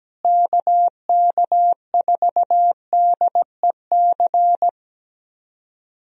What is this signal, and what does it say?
Morse code, heard as a keyed tone, about 17 wpm, 700 Hz. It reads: KK4DEC
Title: USA Call Signs - 6 Characters - Repeated 2x - 17wpm